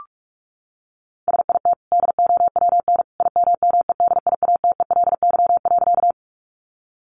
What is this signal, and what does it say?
Morse code, heard as a keyed tone, about 46 wpm, 700 Hz. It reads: SIT DOWN IMMEDIATELY.
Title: Sentences from Top 1100 Words 45wpm